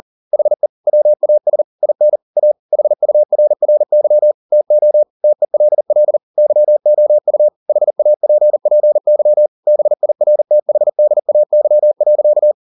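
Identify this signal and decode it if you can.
Morse code, heard as a keyed tone, about 40 wpm, 600 Hz. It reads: HE WAS IN A HURRY TO TELL YOU HAPPY BIRTHDAY.
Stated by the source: Morse code at 40 wpm, practice text, CW med Gnister.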